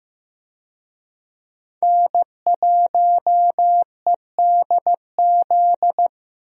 Morse code, heard as a keyed tone, about 15 wpm, 700 Hz. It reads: N1EDZ
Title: USA Call Signs - 5 Characters - Repeated 3x - 15wpm